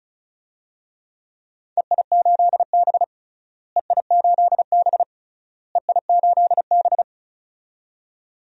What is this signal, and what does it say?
Morse code, heard as a keyed tone, about 34 wpm, 700 Hz. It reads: EI8B EI8B EI8B
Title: DXCC-100 Call Signs - 4 Characters - Repeated 3x - 35wpm